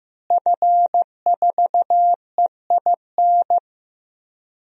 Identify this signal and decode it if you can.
Morse code, heard as a keyed tone, about 15 wpm, 700 Hz. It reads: F4EIN
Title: DXCC-100 Call Signs - 5 Characters - Repeated 2x - 15wpm